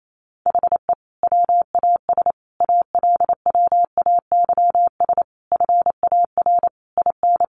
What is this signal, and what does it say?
Morse code, heard as a keyed tone, about 28 wpm, 700 Hz. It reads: HE WAS ALWAYS FAR IN
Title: Sentences from Top 1400 Words 28wpm